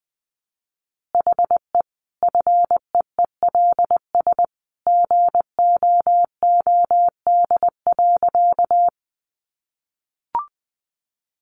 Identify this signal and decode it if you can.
Morse code, heard as a keyed tone, about 20 wpm, 700 Hz. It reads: HE FEELS GOOD.
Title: Additional Easy Phrases 20wpm